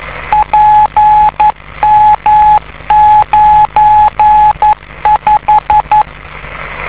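Morse code, heard as a keyed tone, about 11 wpm, 850 Hz, with radio noise. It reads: PM95
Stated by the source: 50 MHz beacon located in Tokyo, Japan.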